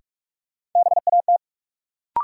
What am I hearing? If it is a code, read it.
Morse code, heard as a keyed tone, about 46 wpm, 700 Hz. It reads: BAT
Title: Sentences from Top 1500 Words - New Word Drill Alt - 45wpm